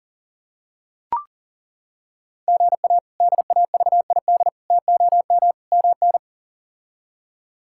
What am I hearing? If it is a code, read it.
Morse code, heard as a keyed tone, about 40 wpm, 700 Hz. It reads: GA DAVID TOM MN
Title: K1USN SST QSO Practice - 40wpm